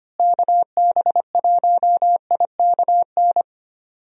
Morse code, heard as a keyed tone, about 25 wpm, 700 Hz. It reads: KB1IKN
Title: USA Call Signs - 6 Characters - Repeated 2x - 25wpm